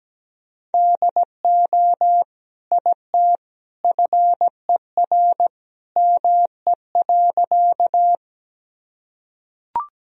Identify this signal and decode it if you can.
Morse code, heard as a keyed tone, about 17 wpm, 700 Hz. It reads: DO IT FER ME.
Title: Sentences from Top 200 Words and Abbreviations 17wpm